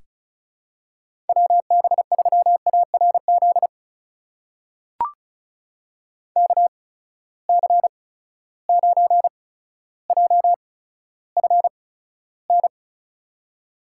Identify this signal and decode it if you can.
Morse code, heard as a keyed tone, about 36 wpm, spaced as for 15 wpm, 700 Hz. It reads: WB3ARZ KC9JFN
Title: USA Call Signs - 6 Characters - 8x Character Spacing - 35wpm